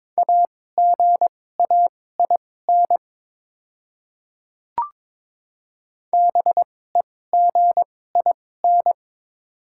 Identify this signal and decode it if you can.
Morse code, heard as a keyed tone, about 22 wpm, spaced as for 16 wpm, 700 Hz. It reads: AGAIN BEGIN
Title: 2x Character Spacing - Top 500 Words - 22wpm